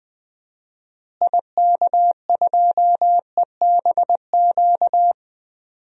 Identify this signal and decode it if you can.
Morse code, heard as a keyed tone, about 20 wpm, 700 Hz. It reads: IK2EBQ
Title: DXCC-100 Call Signs - 6 Characters - Repeated 3x - 20wpm